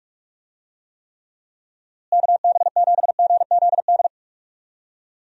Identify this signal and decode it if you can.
Morse code, heard as a keyed tone, about 46 wpm, 700 Hz. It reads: KB7GZD
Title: USA Call Signs - 6 Characters - Repeated 2x - 45wpm